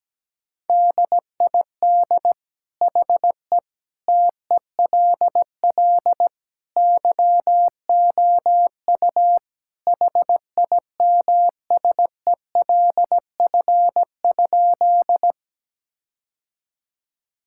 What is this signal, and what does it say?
Morse code, heard as a keyed tone, about 17 wpm, 700 Hz. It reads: DID HE TELL YOU HIMSELF?
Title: Sentences from Top 1600 Words 17wpm